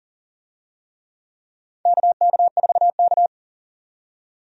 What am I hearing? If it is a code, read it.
Morse code, heard as a keyed tone, about 40 wpm, 700 Hz. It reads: KK4K
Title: USA Call Signs - 4 Characters - 40wpm